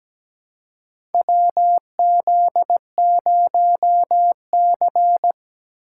Morse code, heard as a keyed tone, about 17 wpm, 700 Hz. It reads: WZ0C
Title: USA Call Signs - 4 Characters - 3x Character Spacing - 17wpm